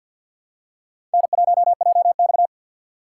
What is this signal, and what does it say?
Morse code, heard as a keyed tone, about 47 wpm, 700 Hz. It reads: N1JX
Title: USA Call Signs - 4 Characters - 6x Character Spacing - 50wpm